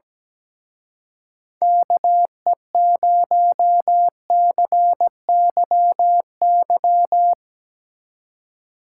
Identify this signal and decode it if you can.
Morse code, heard as a keyed tone, about 19 wpm, 700 Hz. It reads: KE0CYY
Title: Speed Racing - USA 2x3 Call Signs 17wpm